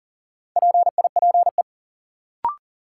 Morse code, heard as a keyed tone, about 40 wpm, 700 Hz. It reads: PIPE
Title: Sentences from Top 1500 Words - New Word Drill - 40wpm